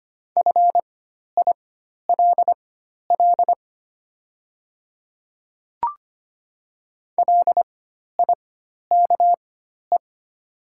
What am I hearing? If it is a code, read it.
Morse code, heard as a keyed tone, about 25 wpm, spaced as for 12 wpm, 690 Hz. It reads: FILL LIKE
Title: 4x Character Spacing - Top 500 Words - 25wpm